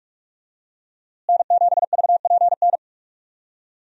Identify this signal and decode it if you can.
Morse code, heard as a keyed tone, about 46 wpm, 700 Hz. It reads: N7VPN
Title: USA Call Signs - 5 Characters - Repeated 2x - 45wpm